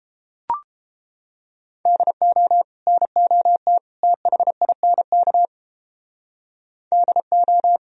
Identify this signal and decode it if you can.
Morse code, heard as a keyed tone, about 29 wpm, 700 Hz. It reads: DO NOT THINK DO